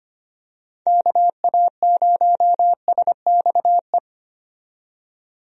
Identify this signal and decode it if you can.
Morse code, heard as a keyed tone, about 25 wpm, 700 Hz. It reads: KA0SXE
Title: USA Call Signs - 6 Characters - 25wpm